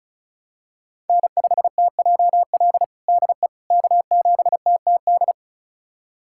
Morse code, heard as a keyed tone, about 34 wpm, 700 Hz. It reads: N5TJL DE K7TTD
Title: USA Callsign DE Callsign - 5 Characters - 35wpm